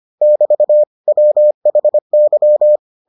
Morse code, heard as a keyed tone, about 25 wpm, 600 Hz. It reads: <BT> WHY